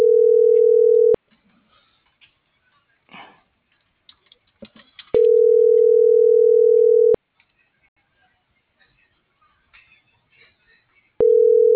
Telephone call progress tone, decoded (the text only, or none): ringback tone